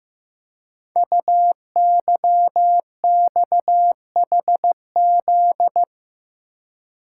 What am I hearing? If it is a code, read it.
Morse code, heard as a keyed tone, about 15 wpm, 700 Hz. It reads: UYXHZ
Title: Code Groups - 5 Letters - 15wpm